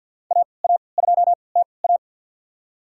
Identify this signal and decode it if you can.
Morse code, heard as a keyed tone, about 53 wpm, spaced as for 29 wpm, 690 Hz. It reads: AA2TA
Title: USA Call Signs - 5 Characters - 3x Character Spacing - 50wpm